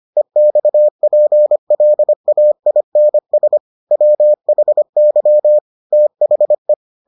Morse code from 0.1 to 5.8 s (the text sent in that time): EXPLAINS WHY